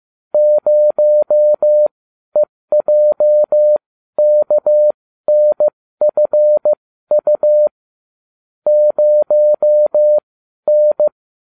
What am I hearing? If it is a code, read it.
Morse code, heard as a keyed tone, about 15 wpm, spaced as for 12 wpm, 600 Hz. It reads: 0EJKNFU 0N